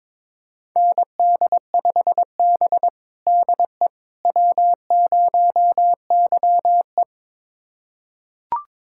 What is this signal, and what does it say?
Morse code, heard as a keyed tone, about 22 wpm, 700 Hz. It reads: ND5B DE W0YE
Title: USA Callsign DE Callsign - 4 Characters - 22wpm